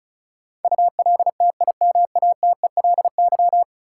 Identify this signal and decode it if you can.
Morse code, heard as a keyed tone, about 36 wpm, 700 Hz. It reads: ULTIMATELY
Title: Sentences from Top 1200 Words - New Word Drill Alt - 35wpm